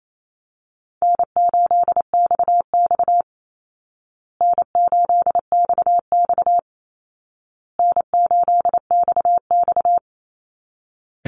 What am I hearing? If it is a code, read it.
Morse code, heard as a keyed tone, about 28 wpm, 700 Hz. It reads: N8XX N8XX N8XX
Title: Call Signs Repeated 2x - 28wpm